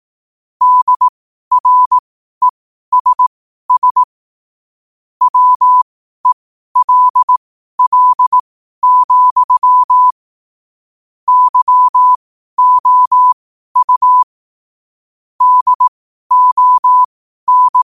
Morse code, heard as a keyed tone, about 18 wpm, spaced as for 12 wpm, 1000 Hz. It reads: DRESS WELL, YOU DON
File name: qotdfiles / 12wpm